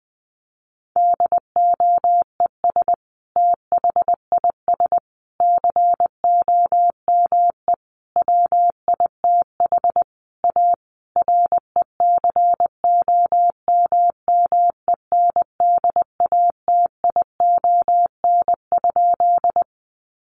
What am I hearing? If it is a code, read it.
Morse code, heard as a keyed tone, about 20 wpm, 700 Hz. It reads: DOES THIS COME WITH A RECOMMENDATION?
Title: Sentences from Top 1400 Words 20wpm